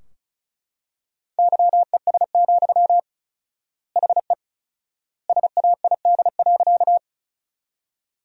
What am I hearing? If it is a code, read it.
Morse code, heard as a keyed tone, about 34 wpm, 700 Hz. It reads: YES, HE SAID.
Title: 4x Word-Spacing - Sentences from Top 500 Words - 35wpm